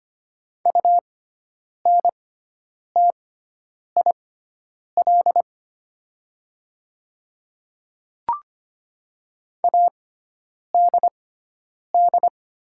Morse code, heard as a keyed tone, about 25 wpm, spaced as for 9 wpm, 700 Hz. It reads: UNTIL ADD
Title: 6x Character Spacing - Top 300 Words - 25wpm